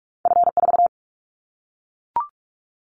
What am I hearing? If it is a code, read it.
Morse code, heard as a keyed tone, about 44 wpm, 700 Hz. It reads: F4
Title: Callsign Prefixes for Top 50 DXCC - 45wpm